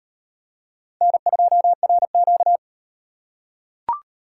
Morse code, heard as a keyed tone, about 38 wpm, 700 Hz. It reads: N2RQ